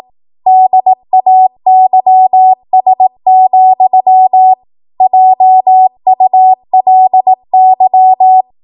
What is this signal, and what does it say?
Morse code, heard as a keyed tone, about 18 wpm, 750 Hz. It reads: DAYS, JULY
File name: Morse / 18WPM